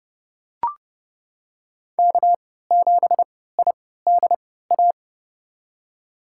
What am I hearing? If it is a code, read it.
Morse code, heard as a keyed tone, about 30 wpm, spaced as for 17 wpm, 700 Hz. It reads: K7IDA
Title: USA Call Signs - 5 Characters - 3x Character Spacing - 30wpm